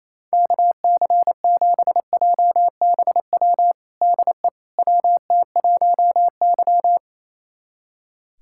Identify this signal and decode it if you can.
Morse code, heard as a keyed tone, about 27 wpm, 700 Hz. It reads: KC7JBW DE WT1Y
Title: USA Callsign DE Callsign 28wpm